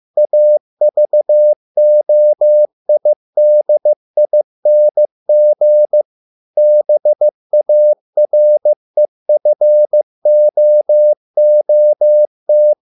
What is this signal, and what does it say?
Morse code, heard as a keyed tone, about 15 wpm, 600 Hz. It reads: AVOIDING BAREFOOT